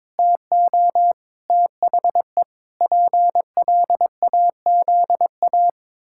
Morse code, heard as a keyed tone, about 22 wpm, 700 Hz. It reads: TO THE PLAZA
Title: Pangram Challenge v2 - 22wpm